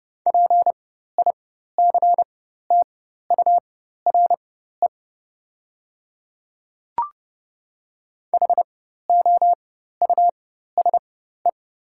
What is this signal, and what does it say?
Morse code, heard as a keyed tone, about 31 wpm, spaced as for 14 wpm, 690 Hz. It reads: PICTURE HOUSE
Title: 4x Character Spacing - Top 300 Words - 30wpm